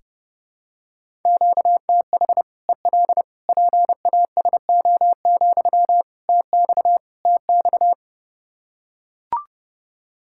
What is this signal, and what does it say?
Morse code, heard as a keyed tone, about 30 wpm, 700 Hz. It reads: QTH EL PASO, TX TX
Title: Practice QSO - Exchange 1 - Beginning - 30wpm